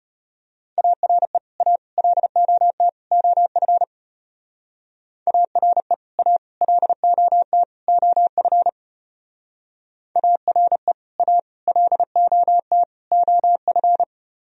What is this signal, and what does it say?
Morse code, heard as a keyed tone, about 33 wpm, 700 Hz. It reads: ARE A LOT OF ARE A LOT OF ARE A LOT OF